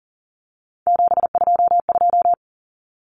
Morse code, heard as a keyed tone, about 40 wpm, 700 Hz. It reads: 722
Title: Common Cut Numbers - 3-Digit 40wpm